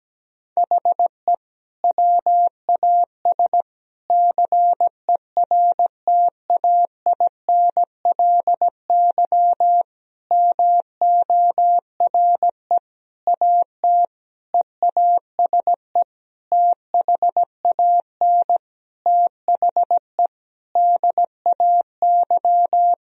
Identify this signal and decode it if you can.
Morse code, heard as a keyed tone, about 17 wpm, 700 Hz. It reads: HE WAS CERTAINLY MORE AT EASE THAN THE DAY
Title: Sentences from Top 1000 Words - Review Part 1 - 17wpm